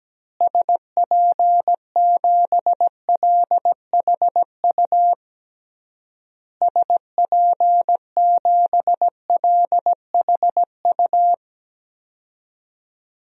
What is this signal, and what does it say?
Morse code, heard as a keyed tone, about 17 wpm, 700 Hz. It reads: SP7LHU SP7LHU
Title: DXCC-100 Call Signs - 6 Characters - Repeated 3x - 17wpm